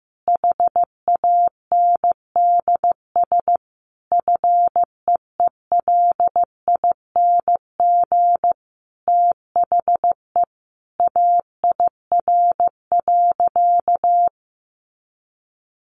Morse code, heard as a keyed tone, about 15 wpm, 700 Hz. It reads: HANDS FEELING THE AIR.
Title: A Sound Of Thunder 15wpm